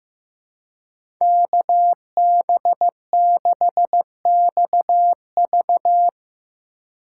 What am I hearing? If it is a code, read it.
Morse code, heard as a keyed tone, about 15 wpm, 700 Hz. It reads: KB6XV
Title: USA Call Signs - 2x Character Spacing - 15wpm